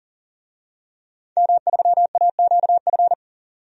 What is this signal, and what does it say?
Morse code, heard as a keyed tone, about 40 wpm, 700 Hz. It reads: M3AQF
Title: DXCC-100 Call Signs - 5 Characters - 40wpm